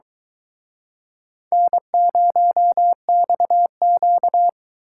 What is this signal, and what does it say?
Morse code, heard as a keyed tone, about 24 wpm, 700 Hz. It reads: N0XQ